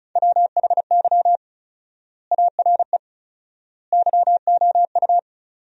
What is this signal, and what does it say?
Morse code, heard as a keyed tone, about 34 wpm, 700 Hz. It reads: WHY ARE YOU
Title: 4x Word-Spacing - Sentences from Top 200 Words - 35wpm